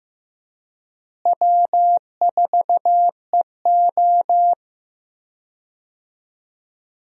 Morse code, heard as a keyed tone, about 15 wpm, 700 Hz. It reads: W4EO
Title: USA Call Signs - 4 Characters - Repeated 3x - 15wpm